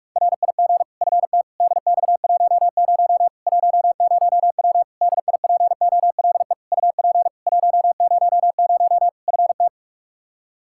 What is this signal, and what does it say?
Morse code, heard as a keyed tone, about 41 wpm, 700 Hz. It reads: RIG FT DX10 10W DIPOLE UP 100 FT
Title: Practice QSO - Exchange 2 - Info - 45wpm